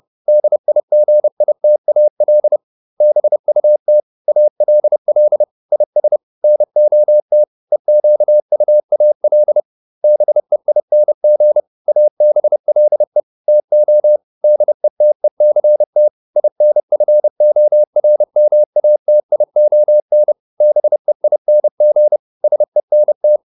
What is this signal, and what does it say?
Morse code, heard as a keyed tone, about 30 wpm, 600 Hz. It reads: DIGITAL BUT ALL IS NOT EQUAL BEING ABLE TO DETECT INFORMATION BEING SENT